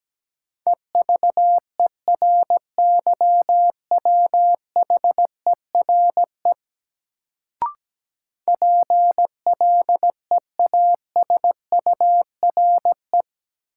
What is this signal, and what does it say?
Morse code, heard as a keyed tone, about 17 wpm, 700 Hz. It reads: EVERYWHERE PLEASURE
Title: Sentences from Top 1600 Words - New Word Drill - 17wpm